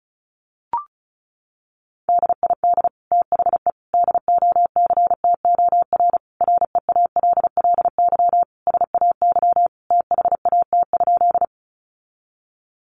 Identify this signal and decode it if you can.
Morse code, heard as a keyed tone, about 34 wpm, 700 Hz. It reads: DID THE DOCTOR REALLY SAY THAT?
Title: Sentences from Top 1500 Words 35wpm